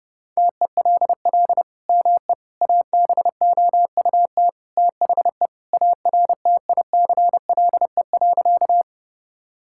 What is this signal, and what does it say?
Morse code, heard as a keyed tone, about 30 wpm, 700 Hz. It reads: TELL ME ABOUT THE ARTICLE.
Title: Sentences from Top 1000 Words - Review Part 5 - 30wpm